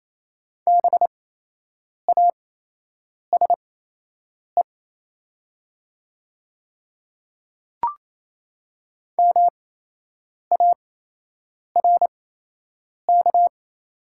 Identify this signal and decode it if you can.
Morse code, heard as a keyed tone, about 27 wpm, spaced as for 8 wpm, 700 Hz. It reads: BASE MARK
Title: 8x Character Spacing - Top 300 Words - 28wpm